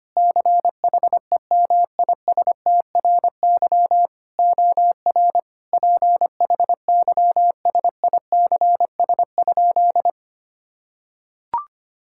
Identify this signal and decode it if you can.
Morse code, heard as a keyed tone, about 25 wpm, 700 Hz. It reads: CHEMISTRY OR PHYSICS?
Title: Sentences from Top 1000 Words - Review Part 6 - 25wpm